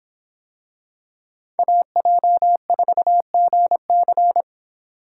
Morse code, heard as a keyed tone, about 24 wpm, 700 Hz. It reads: AJ4GC